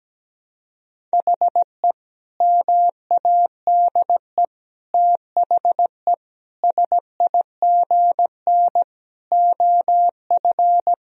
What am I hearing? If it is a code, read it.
Morse code, heard as a keyed tone, about 17 wpm, 700 Hz. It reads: HE MADE THE SIGN OF
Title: Sentences from Top 1000 Words - Review Part 1 - 17wpm